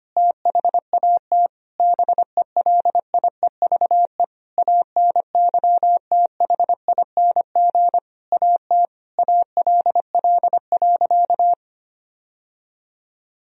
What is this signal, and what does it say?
Morse code, heard as a keyed tone, about 25 wpm, 700 Hz. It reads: THAT BELIEVE ANYTHING AT ALL.
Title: Sentences from Top 1000 Words - Review Part 1 - 25wpm